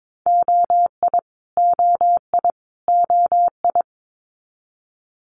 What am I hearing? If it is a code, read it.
Morse code, heard as a keyed tone, about 22 wpm, 700 Hz. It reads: OI OI OI
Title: Multi-letter Phonemes Repeated 3x 22wpm